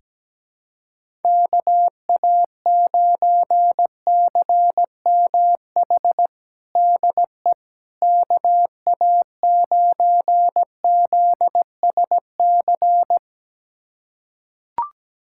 Morse code, heard as a keyed tone, about 17 wpm, 700 Hz. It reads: KA9CMH DE KA9ZSC
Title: USA Callsign DE Callsign 17wpm